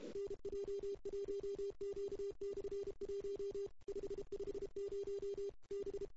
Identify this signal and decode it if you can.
Morse code, heard as a keyed tone, about 31 wpm, 400 Hz. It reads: NJ1Q/1 550 B
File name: CW / 2008-ss-cw-k5tr-op-n5rz -02:00